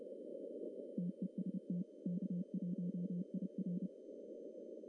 Morse code, heard as a keyed tone, about 29 wpm, 190 Hz. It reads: TEST K1IR
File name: CW / cqww-cw-k5tr